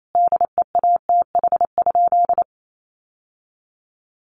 Morse code, heard as a keyed tone, about 28 wpm, 700 Hz. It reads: DEATH?
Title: Sentences from Top 1100 Words 28wpm